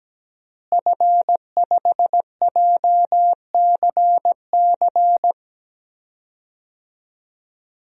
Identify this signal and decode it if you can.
Morse code, heard as a keyed tone, about 17 wpm, 700 Hz. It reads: F5JCC
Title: DXCC-100 Call Signs - 5 Characters - Repeated 3x - 17wpm